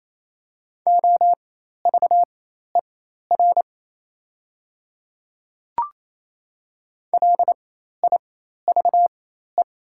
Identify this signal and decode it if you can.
Morse code, heard as a keyed tone, about 28 wpm, spaced as for 13 wpm, 690 Hz. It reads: OVER LIVE
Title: 4x Character Spacing - Top 200 Words - 28wpm